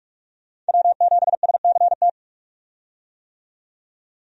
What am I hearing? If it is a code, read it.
Morse code, heard as a keyed tone, about 43 wpm, 700 Hz. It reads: W7SCT
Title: USA Call Signs - 5 Characters - 45wpm